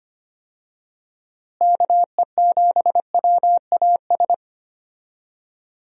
Morse code, heard as a keyed tone, about 25 wpm, 700 Hz. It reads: KE7WAS
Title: USA Call Signs - 6 Characters - Repeated 3x - 25wpm